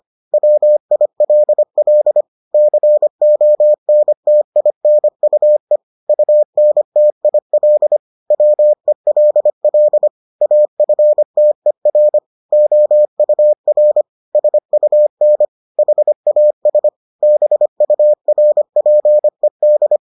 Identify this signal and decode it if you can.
Morse code, heard as a keyed tone, about 25 wpm, 600 Hz. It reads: WILL CONTINUE UNTIL WELL AFTER OUR SUN HAS BURPED